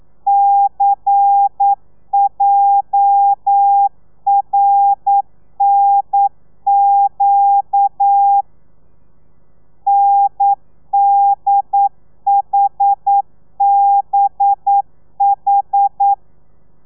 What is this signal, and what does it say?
Morse code, heard as a keyed tone, about 9 wpm, 800 Hz. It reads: CJRNQ NDHBH
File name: telegrafia / i2tfj / Lez11 40